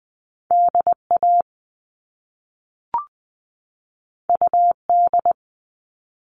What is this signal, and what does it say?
Morse code, heard as a keyed tone, about 20 wpm, 700 Hz. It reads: DA UD
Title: All Cut Numbers - 2-Digit 20wpm